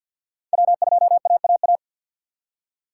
Morse code, heard as a keyed tone, about 53 wpm, 690 Hz. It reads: W2AAA
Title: USA Call Signs - 5 Characters - 6x Character Spacing - 50wpm